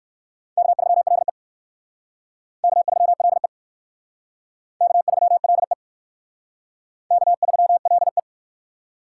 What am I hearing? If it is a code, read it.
Morse code, heard as a keyed tone, about 62 wpm, 700 Hz. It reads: K3LE K3LE K3LE K3LE